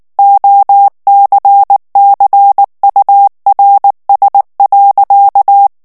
Morse code, heard as a keyed tone, about 19 wpm, 800 Hz. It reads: OCCURS.
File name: ANGLAIS texte clair 3 vitesse 18 mots minute